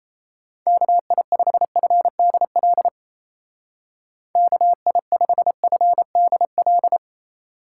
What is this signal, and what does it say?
Morse code, heard as a keyed tone, about 35 wpm, 700 Hz. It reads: KI5FDL KI5FDL
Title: Speed Racing - USA 2x3 Call Signs 28wpm